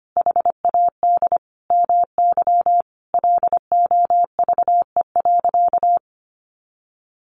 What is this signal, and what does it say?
Morse code, heard as a keyed tone, about 25 wpm, 700 Hz. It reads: HAD MY LOVE.
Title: Sentences from Top 400 Words 25wpm